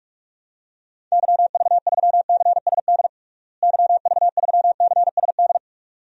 Morse code, heard as a keyed tone, about 46 wpm, 700 Hz. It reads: YV3CSD YV3CSD
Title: DXCC-100 Call Signs - 6 Characters - Repeated 2x - 45wpm